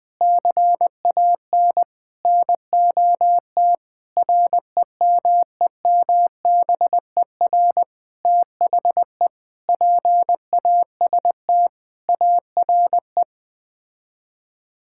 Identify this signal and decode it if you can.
Morse code, heard as a keyed tone, about 21 wpm, 700 Hz. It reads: CAN NOT REMEMBER THE PAST ARE
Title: Sets of 8 Words - 20wpm